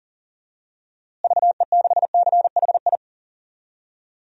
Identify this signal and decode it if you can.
Morse code, heard as a keyed tone, about 40 wpm, 700 Hz. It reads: VE6CHI
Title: DXCC-100 Call Signs - 6 Characters - Repeated 2x - 40wpm